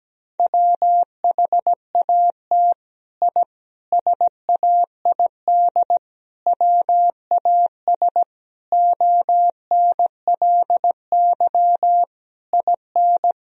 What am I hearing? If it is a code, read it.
Morse code, heard as a keyed tone, about 17 wpm, 700 Hz. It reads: WHAT I SAID WAS ONLY IN
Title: Sentences from Top 1000 Words - Review Part 4 - 17wpm